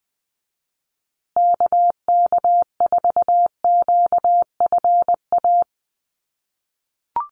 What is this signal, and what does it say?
Morse code, heard as a keyed tone, about 20 wpm, 700 Hz. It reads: KK4QFA
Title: USA Call Signs Repeated 3x 20wpm